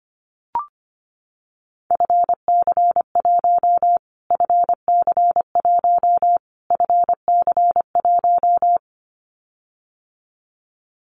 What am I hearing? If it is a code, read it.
Morse code, heard as a keyed tone, about 25 wpm, 700 Hz. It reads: FC1 FC1 FC1
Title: Callsign Prefixes for Top 25 DXCC - Repeated 3x - 25wpm